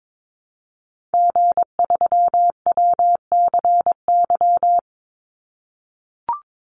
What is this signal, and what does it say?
Morse code, heard as a keyed tone, about 22 wpm, 700 Hz. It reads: G3WCY
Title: Call Signs - 22wpm